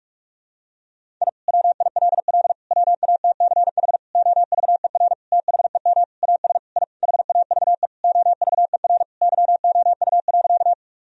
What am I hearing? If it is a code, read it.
Morse code, heard as a keyed tone, about 41 wpm, 700 Hz. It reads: I WILL WATCH OVER THEM AS I HAVE OVER YOU.
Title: Sentences from Top 1000 Words - Review Part 1 - 45wpm